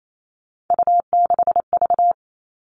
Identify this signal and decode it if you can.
Morse code, heard as a keyed tone, about 28 wpm, 700 Hz. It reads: U6V
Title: All Cut Numbers - 3-Digit 28wpm